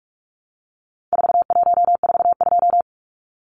Morse code, heard as a keyed tone, about 44 wpm, 700 Hz. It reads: 4142
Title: Numbers - 4-Digit 45wpm